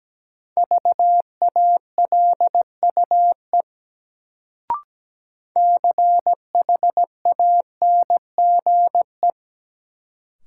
Morse code, heard as a keyed tone, about 17 wpm, 700 Hz. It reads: VALUE CHANGE
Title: Sentences from Top 600 Words - New Word Drill - 17wpm